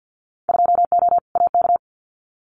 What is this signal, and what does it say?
Morse code, heard as a keyed tone, about 50 wpm, 700 Hz. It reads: 2O AK
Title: Winter Field Day Exchanges Encore 50wpm